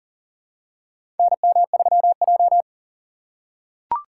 Morse code, heard as a keyed tone, about 40 wpm, 700 Hz. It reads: NM3J
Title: USA Call Signs - 4 Characters - 3x Character Spacing - 40wpm